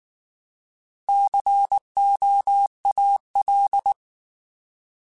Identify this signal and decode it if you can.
Morse code, heard as a keyed tone, about 19 wpm, 800 Hz. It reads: COAL
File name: Alphabetical / Alphabetical mp3s / Cs @18